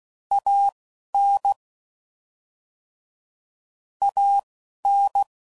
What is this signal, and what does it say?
Morse code, heard as a keyed tone, about 16 wpm, spaced as for 11 wpm, 800 Hz. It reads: AN AN
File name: Mixed / Short Runs / ShortContrasts